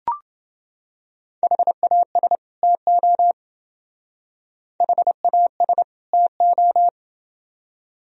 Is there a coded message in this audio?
Yes: Morse code from 1.4 to 6.9 s, heard as a keyed tone, about 28 wpm, 700 Hz. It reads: HAS TO HAS TO